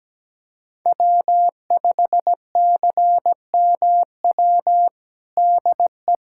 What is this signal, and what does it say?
Morse code, heard as a keyed tone, about 17 wpm, 700 Hz. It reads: W5CMW DE
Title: USA Callsign DE Callsign 17wpm